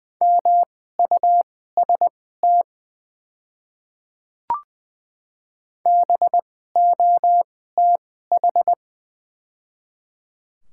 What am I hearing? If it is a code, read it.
Morse code, heard as a keyed tone, about 20 wpm, spaced as for 15 wpm, 700 Hz. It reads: MUST BOTH
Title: 2x Character Spacing - Top 300 Words - 20wpm